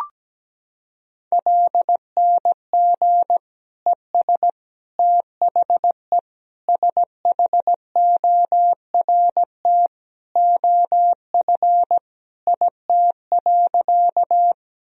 Morse code, heard as a keyed tone, about 17 wpm, 700 Hz. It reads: LNG ES THE SHORT OF IT.
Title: Sentences from Top 400 Words and Abbreviations 17wpm